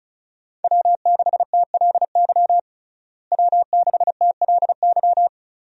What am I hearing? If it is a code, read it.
Morse code, heard as a keyed tone, about 36 wpm, 700 Hz. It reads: W6TLY W6TLY
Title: USA Call Signs - 5 Characters - Repeated 3x - 35wpm